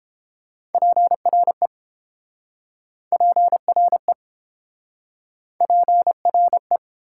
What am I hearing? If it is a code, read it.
Morse code, heard as a keyed tone, about 31 wpm, 700 Hz. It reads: PRE PRE PRE